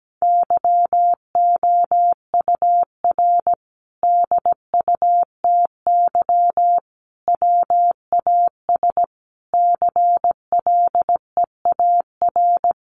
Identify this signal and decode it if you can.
Morse code, heard as a keyed tone, about 17 wpm, 700 Hz. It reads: YOUR DUTY WAS CLEAR
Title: Sentences from Top 1500 Words 17wpm